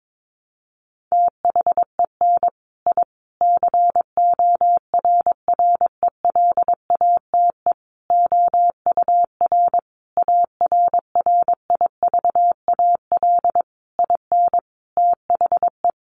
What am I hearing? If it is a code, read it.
Morse code, heard as a keyed tone, about 22 wpm, 700 Hz. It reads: THEN I CORRELATE OUR ARRIVAL IN THE
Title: A Sound Of Thunder 22wpm